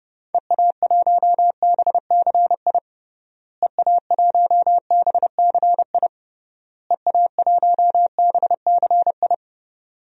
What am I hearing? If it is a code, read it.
Morse code, heard as a keyed tone, about 30 wpm, 700 Hz. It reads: EA1BCI EA1BCI EA1BCI
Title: DXCC-100 Call Signs - 6 Characters - Repeated 3x - 30wpm